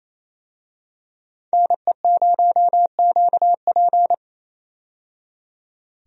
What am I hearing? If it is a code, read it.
Morse code, heard as a keyed tone, about 28 wpm, 700 Hz. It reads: NE0QP
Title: USA Call Signs - 5 Characters - 28wpm